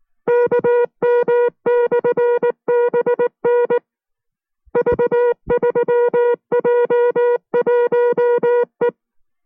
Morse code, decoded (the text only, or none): KM/BN V3J1E